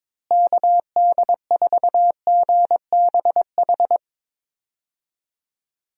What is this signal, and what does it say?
Morse code, heard as a keyed tone, about 22 wpm, 700 Hz. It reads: KD4GBH
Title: USA Call Signs - 6 Characters - Repeated 2x - 22wpm